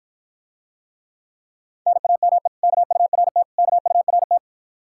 Morse code, heard as a keyed tone, about 50 wpm, 700 Hz. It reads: NAME KURT KURT